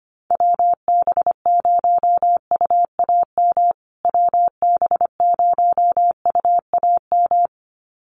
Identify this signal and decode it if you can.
Morse code, heard as a keyed tone, about 25 wpm, 700 Hz. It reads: WB0UAM WB0UAM
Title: USA Call Signs Repeated 2x 25wpm